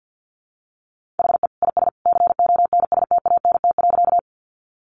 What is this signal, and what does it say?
Morse code, heard as a keyed tone, about 50 wpm, 700 Hz. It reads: HE IS CONSTANT.
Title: Sentences from Top 1500 Words 50wpm